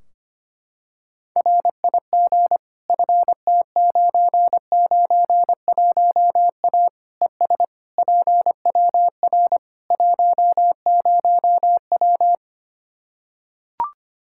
Morse code, heard as a keyed tone, about 25 wpm, 700 Hz. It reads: RIG FT991A ES PWR 10W
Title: Practice QSO - Exchange 2 - Info - 25wpm